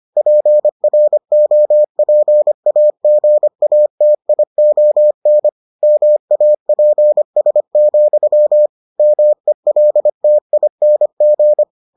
Morse code, heard as a keyed tone, about 25 wpm, 600 Hz. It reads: PROPAGATION MAPS, MELTING